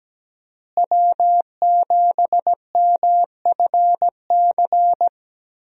Morse code, heard as a keyed tone, about 17 wpm, 700 Hz. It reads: W7MFC
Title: USA Call Signs - 5 Characters - Repeated 3x - 17wpm